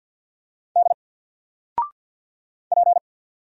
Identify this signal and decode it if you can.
Morse code, heard as a keyed tone, about 49 wpm, 700 Hz. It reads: D P
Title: Single Letter-Number 50wpm